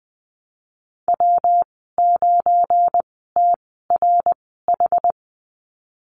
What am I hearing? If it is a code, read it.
Morse code, heard as a keyed tone, about 20 wpm, spaced as for 15 wpm, 700 Hz. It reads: W9TRH
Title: 213 Lesson 34g - Number 8 - Call Signs 20wpm